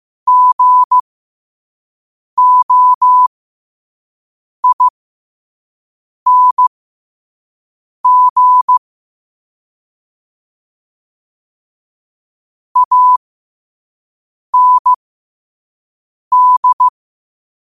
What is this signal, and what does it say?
Morse code, heard as a keyed tone, about 15 wpm, spaced as for 5 wpm, 1000 Hz. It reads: GOING AND
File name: qotdfiles / 5wpm